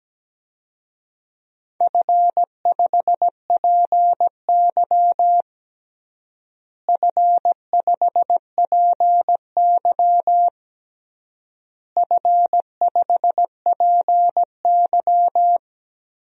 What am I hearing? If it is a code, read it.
Morse code, heard as a keyed tone, about 17 wpm, 700 Hz. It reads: F5PY F5PY F5PY
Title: DXCC-100 Call Signs - 4 Characters - Repeated 3x - 17wpm